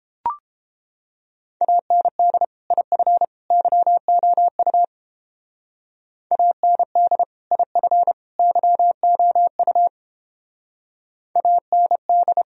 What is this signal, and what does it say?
Morse code, heard as a keyed tone, about 29 wpm, 700 Hz. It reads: AND IF YOU AND IF YOU AND